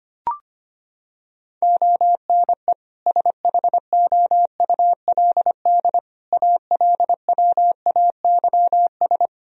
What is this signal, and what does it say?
Morse code, heard as a keyed tone, about 25 wpm, 700 Hz. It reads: ONE SHOULD ALWAYS
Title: Sentences from Top 1000 Words - Review Part 3 - 25wpm